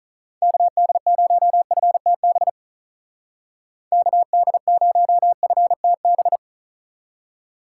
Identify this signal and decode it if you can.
Morse code, heard as a keyed tone, about 44 wpm, 700 Hz. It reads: KD0FTB KD0FTB
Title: Speed Racing - USA 2x3 Call Signs 35wpm